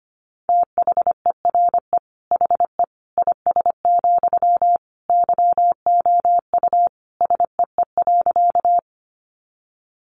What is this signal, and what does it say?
Morse code, heard as a keyed tone, about 25 wpm, 700 Hz. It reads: THERE HE IS, YOU SEE.
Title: Sentences from Top 100 Words 25wpm